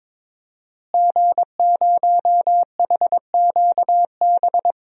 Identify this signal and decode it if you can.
Morse code, heard as a keyed tone, about 21 wpm, 700 Hz. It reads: G0HQB
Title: DXCC-100 Call Signs - 5 Characters - Repeated 3x - 22wpm